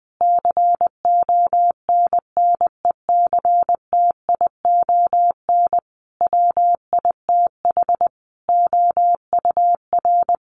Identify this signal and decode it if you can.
Morse code, heard as a keyed tone, about 20 wpm, 700 Hz. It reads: CONNECTION WITH OUR
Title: Sentences from Top 1300 Words 20wpm